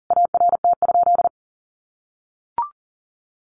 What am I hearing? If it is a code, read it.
Morse code, heard as a keyed tone, about 40 wpm, 700 Hz. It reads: ART?
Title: Sentences from Top 1400 Words 40wpm